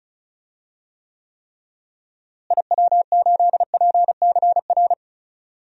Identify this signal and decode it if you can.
Morse code, heard as a keyed tone, about 36 wpm, 700 Hz. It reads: IW8PCR
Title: DXCC-100 Call Signs - 6 Characters - Repeated 3x - 35wpm